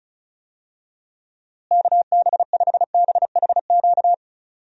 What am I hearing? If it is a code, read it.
Morse code, heard as a keyed tone, about 34 wpm, 700 Hz. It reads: KB5BHQ
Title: USA Call Signs - 6 Characters - Repeated 3x - 35wpm